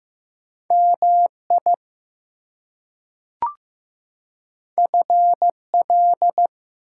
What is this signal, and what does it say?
Morse code, heard as a keyed tone, about 15 wpm, 700 Hz. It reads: MI FL
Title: US States and Canadian Provinces 15wpm